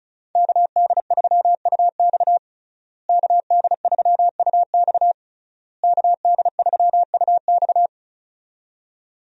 Morse code, heard as a keyed tone, about 36 wpm, 700 Hz. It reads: KD3UX KD3UX KD3UX
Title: USA Call Signs - 5 Characters - Repeated 3x - 35wpm